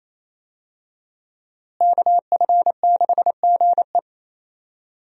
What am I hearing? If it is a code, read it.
Morse code, heard as a keyed tone, about 28 wpm, 700 Hz. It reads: KF6GE
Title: USA Call Signs - 5 Characters - Repeated 2x - 28wpm